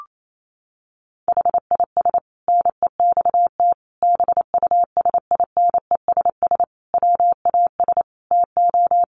Morse code, heard as a keyed tone, about 28 wpm, 700 Hz. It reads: HIS NEXT BUSINESS WAS TO
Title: Sentences from Top 1300 Words 28wpm